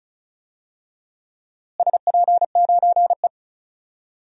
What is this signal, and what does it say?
Morse code, heard as a keyed tone, about 34 wpm, 700 Hz. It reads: SP9E
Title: DXCC-100 Call Signs - 4 Characters - Repeated 3x - 35wpm